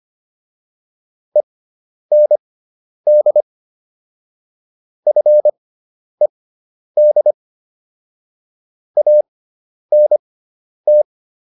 Morse code, heard as a keyed tone, about 25 wpm, spaced as for 10 wpm, 600 Hz. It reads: END FED ANT